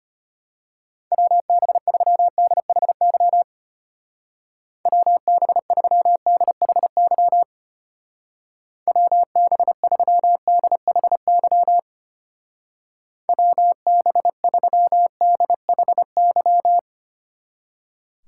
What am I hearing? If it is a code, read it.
Morse code, heard as a keyed tone, about 31 wpm, 700 Hz. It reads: WB3DHY WB3DHY WB3DHY WB3DHY
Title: Speed Racing - USA Call Signs 25wpm